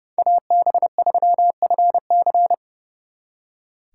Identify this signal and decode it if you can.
Morse code, heard as a keyed tone, about 32 wpm, 700 Hz. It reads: AB3FC
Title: Speed Racing - USA 2x2 Call Signs 30wpm